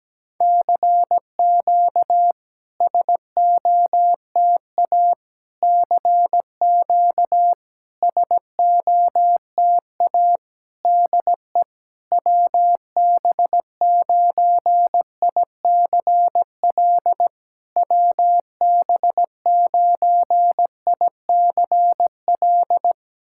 Morse code, heard as a keyed tone, about 17 wpm, 700 Hz. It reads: CQ SOTA CQ SOTA DE WB9ICL WB9ICL
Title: SOTA QSO Practice 17wpm